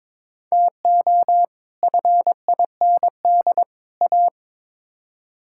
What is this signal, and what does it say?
Morse code, heard as a keyed tone, about 22 wpm, 700 Hz. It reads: TO FIND A
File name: practice / 22wpm / Build A Phrase - 5-Word Phrases - 22wpm